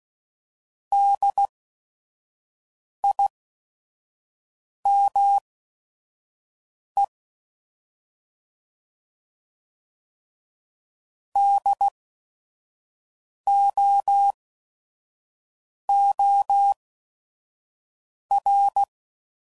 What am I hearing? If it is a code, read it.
Morse code, heard as a keyed tone, about 16 wpm, spaced as for 5 wpm, 800 Hz. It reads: DIME DOOR
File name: Prefixes / By First Letter / Ds